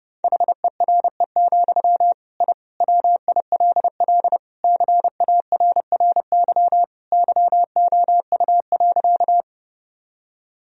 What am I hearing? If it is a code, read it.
Morse code, heard as a keyed tone, about 30 wpm, 700 Hz. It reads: HERE, I WILL CARRY YOU.
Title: Sentences from Top 1000 Words - Review Part 1 - 30wpm